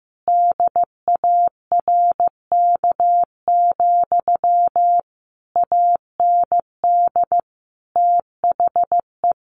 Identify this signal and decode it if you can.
Morse code, heard as a keyed tone, about 15 wpm, 700 Hz. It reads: DARK, AND THE
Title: Sentences from Top 1200 Words 15wpm